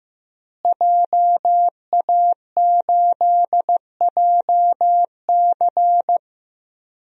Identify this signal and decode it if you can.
Morse code, heard as a keyed tone, about 15 wpm, 700 Hz. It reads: JA8JC
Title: DXCC-100 Call Signs - 5 Characters - Repeated 2x - 15wpm